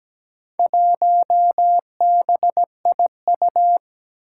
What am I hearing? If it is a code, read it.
Morse code, heard as a keyed tone, about 17 wpm, 700 Hz. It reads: 1BIU
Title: Four Challenging Letter-Number Combinations 17wpm